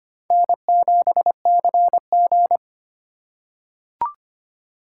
Morse code, heard as a keyed tone, about 25 wpm, 700 Hz. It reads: N7CG
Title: USA Call Signs - 8x Character Spacing - 25wpm